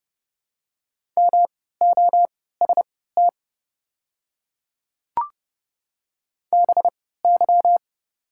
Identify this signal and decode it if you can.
Morse code, heard as a keyed tone, about 30 wpm, spaced as for 17 wpm, 700 Hz. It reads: MOST BY
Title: 3x Character Spacing - Top 200 Words - 30wpm